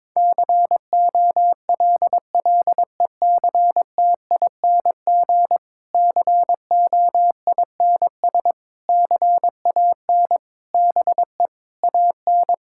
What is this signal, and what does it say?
Morse code, heard as a keyed tone, about 22 wpm, 700 Hz. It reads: COLLECTING COINS CAN BE AN
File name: Difficult Sentences with Hints - 22wpm